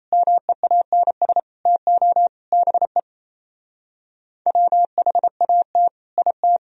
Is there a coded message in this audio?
Yes: Morse code from 0.1 to 6.6 s, heard as a keyed tone, about 37 wpm, 700 Hz. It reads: MEANS TO BE WHAT IT